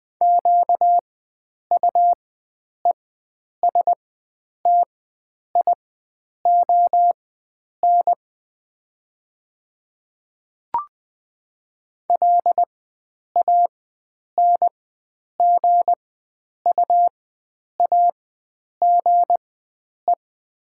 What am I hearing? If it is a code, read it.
Morse code, heard as a keyed tone, about 20 wpm, spaced as for 9 wpm, 700 Hz. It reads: QUESTION LANGUAGE
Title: 4x Character Spacing - Top 500 Words - 20wpm